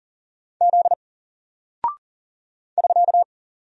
Morse code, heard as a keyed tone, about 40 wpm, 700 Hz. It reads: Z <SK>
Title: Single Character (Letter:Number:Punctuation:Prosign) Alt 40wpm